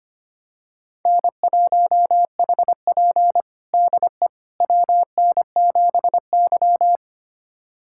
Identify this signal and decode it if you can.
Morse code, heard as a keyed tone, about 25 wpm, 700 Hz. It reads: N1HP DE WN7Y
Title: USA Callsign DE Callsign - 4 Characters - 25wpm